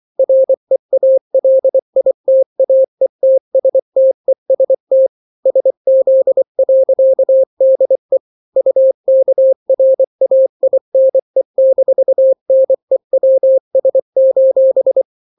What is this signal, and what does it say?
Morse code, heard as a keyed tone, about 24 wpm, 530 Hz. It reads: REALITAETSTEST SZ.DE UKRAINE-NEWS:
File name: daily / 23wpm